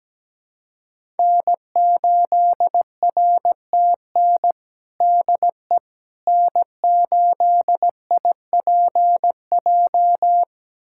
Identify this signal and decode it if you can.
Morse code, heard as a keyed tone, about 17 wpm, 700 Hz. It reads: N8RTN DE N8IPJ
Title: USA Callsign DE Callsign - 5 Characters - 17wpm